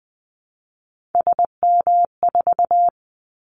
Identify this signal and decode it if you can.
Morse code, heard as a keyed tone, about 20 wpm, 700 Hz. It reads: SM4
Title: Callsign Prefixes for Top 25 DXCC - Repeated 3x - 20wpm